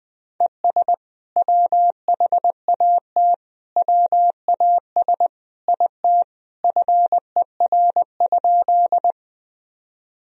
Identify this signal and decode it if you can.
Morse code, heard as a keyed tone, about 21 wpm, 700 Hz. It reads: ES WHAT WAS IT FER?
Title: Sentences from Top 100 Words and Abbreviations 20wpm